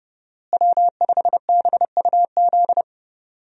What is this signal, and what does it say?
Morse code, heard as a keyed tone, about 31 wpm, 700 Hz. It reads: W5BUZ
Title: USA Call Signs - 5 Characters - Repeated 2x - 30wpm